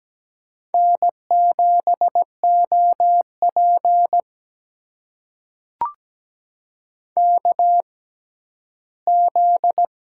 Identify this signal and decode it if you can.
Morse code, heard as a keyed tone, about 17 wpm, spaced as for 12 wpm, 700 Hz. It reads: N7OP KZ
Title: USA Call Signs - 4 Characters - 6x Character Spacing - 17wpm